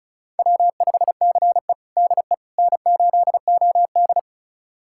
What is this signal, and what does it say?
Morse code, heard as a keyed tone, about 34 wpm, 700 Hz. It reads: W5CE DE N8OD
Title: USA Callsign DE Callsign - 4 Characters - 35wpm